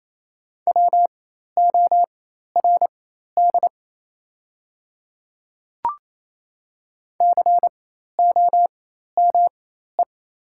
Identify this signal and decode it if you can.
Morse code, heard as a keyed tone, about 28 wpm, spaced as for 13 wpm, 700 Hz. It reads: WORD COME
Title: 4x Character Spacing - Top 100 Words - 28wpm